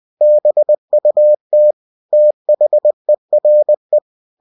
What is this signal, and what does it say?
Morse code, heard as a keyed tone, about 20 wpm, 600 Hz. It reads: BUT THERE
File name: cw-books / winnie-the-pooh / winnie-20wpm-page-0087